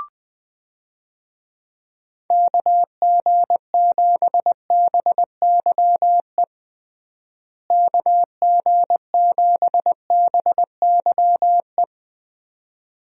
Morse code, heard as a keyed tone, about 20 wpm, 700 Hz. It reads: KG7BYE KG7BYE
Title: USA Call Signs - 6 Characters - Repeated 2x - 20wpm